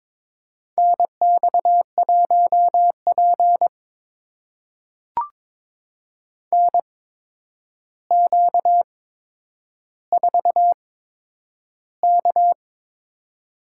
Morse code, heard as a keyed tone, about 22 wpm, spaced as for 9 wpm, 700 Hz. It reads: NX1P NQ4K
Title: USA Call Signs - 4 Characters - 8x Character Spacing - 22wpm